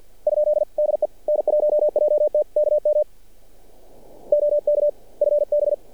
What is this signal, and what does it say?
Morse code, heard as a keyed tone, about 46 wpm, 590 Hz. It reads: ? DE N1JT KM OK FB
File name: archived / CRSnet